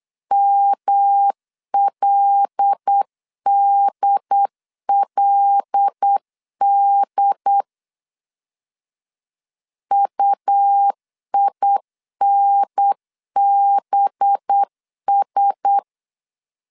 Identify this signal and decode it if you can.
Morse code, heard as a keyed tone, about 8 wpm, 800 Hz. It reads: MLDLD UINBS
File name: 8wpm